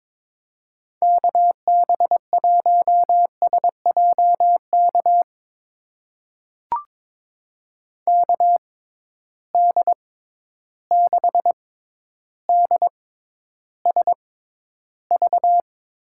Morse code, heard as a keyed tone, about 22 wpm, spaced as for 11 wpm, 700 Hz. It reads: KB1SJK KD6DSV
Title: USA Call Signs - 6 Characters - 6x Character Spacing - 22wpm